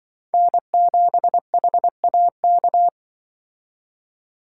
Morse code, heard as a keyed tone, about 25 wpm, 700 Hz. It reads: N7HAK